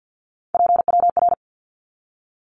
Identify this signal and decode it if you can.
Morse code, heard as a keyed tone, about 50 wpm, 700 Hz. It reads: PWR
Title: Abbreviations 50wpm